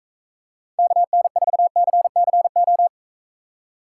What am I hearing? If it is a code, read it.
Morse code, heard as a keyed tone, about 42 wpm, 700 Hz. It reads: KN4CCO